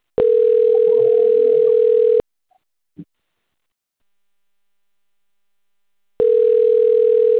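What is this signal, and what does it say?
Ringback tone (North America), 440 + 480 Hz.
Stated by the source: Outbound telephone call from a system at Bebedouro 2.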